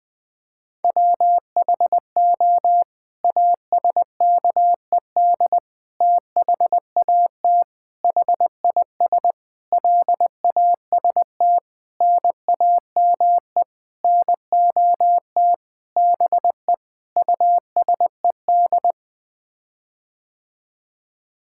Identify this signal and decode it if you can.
Morse code, heard as a keyed tone, about 21 wpm, 700 Hz. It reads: WHO ASKED THAT HIS LAST NAME NOT BE USED
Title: Sets of 9 Words - 20wpm